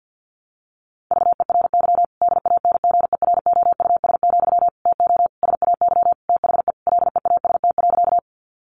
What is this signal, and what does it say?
Morse code, heard as a keyed tone, about 50 wpm, 700 Hz. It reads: VERY DANGEROUS, TO SAY THE LEAST.
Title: Sentences from Top 1200 Words 50wpm